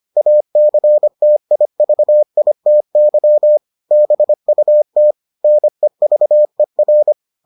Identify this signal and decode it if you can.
Morse code, heard as a keyed tone, about 25 wpm, 600 Hz. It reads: ACTIVITY BUT NEVER